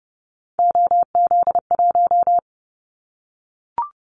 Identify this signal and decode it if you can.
Morse code, heard as a keyed tone, about 30 wpm, 700 Hz. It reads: OZ1
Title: Callsign Prefixes for Top 50 DXCC - 30wpm